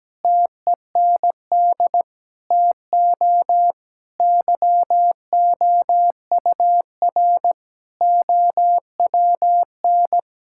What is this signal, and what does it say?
Morse code, heard as a keyed tone, about 17 wpm, 700 Hz. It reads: TEND TO YOUR OWN
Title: Sentences from Top 1000 Words - Review Part 5 - 17wpm